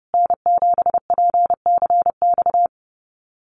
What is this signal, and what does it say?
Morse code, heard as a keyed tone, about 30 wpm, 700 Hz. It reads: N7PCX
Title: USA Call Signs Repeated 3x 30wpm